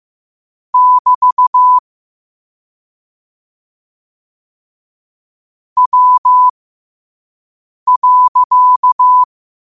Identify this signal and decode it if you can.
Morse code, heard as a keyed tone, about 15 wpm, spaced as for 5 wpm, 1000 Hz. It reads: <BT> W.
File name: qotdfiles / 5wpm